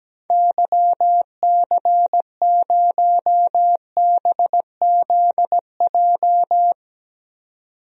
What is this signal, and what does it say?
Morse code, heard as a keyed tone, about 17 wpm, 700 Hz. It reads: YC0BZJ
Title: DXCC-100 Call Signs - 6 Characters - 17wpm